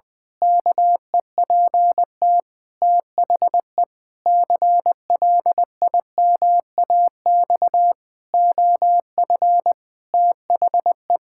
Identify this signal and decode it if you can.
Morse code, heard as a keyed tone, about 20 wpm, 700 Hz. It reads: KEPT THE CLIMAX OF THE
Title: Pangram Challenge v2 - 20wpm